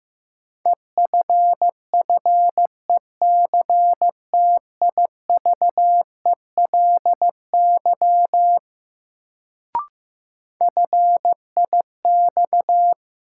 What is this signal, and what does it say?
Morse code, heard as a keyed tone, about 15 wpm, 700 Hz. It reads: EFFECTIVELY FIX
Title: Sentences from Top 1400 Words - New Word Drill Alt - 15wpm